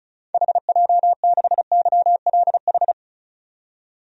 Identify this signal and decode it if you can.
Morse code, heard as a keyed tone, about 34 wpm, 700 Hz. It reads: HJ6YLH
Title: DXCC-100 Call Signs - 6 Characters - 35wpm